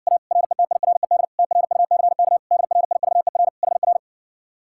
Morse code, heard as a keyed tone, about 59 wpm, 700 Hz. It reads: A RETIRED TRUCK DRIVER HR